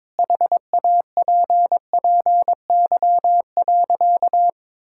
Morse code, heard as a keyed tone, about 22 wpm, 700 Hz. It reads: HAPPY.
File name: practice / 22wpm / Sentences with 3-Word Phrases - 22wpm